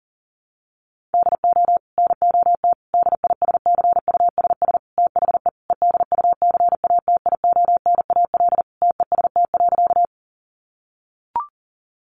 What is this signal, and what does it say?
Morse code, heard as a keyed tone, about 40 wpm, 700 Hz. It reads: DO NOT DISCUSS THE EDUCATIONAL TEST.
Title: Sentences from Top 1200 Words 40wpm